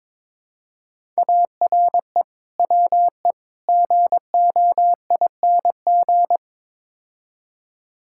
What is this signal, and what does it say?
Morse code, heard as a keyed tone, about 22 wpm, 700 Hz. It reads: ARE WE GOING
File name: practice / 22wpm / Build A Phrase - 5-Word Phrases - 22wpm